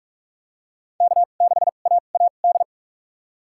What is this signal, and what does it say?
Morse code, heard as a keyed tone, about 46 wpm, spaced as for 32 wpm, 700 Hz. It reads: K6AAD
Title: USA Call Signs - 5 Characters - 2x Character Spacing - 45wpm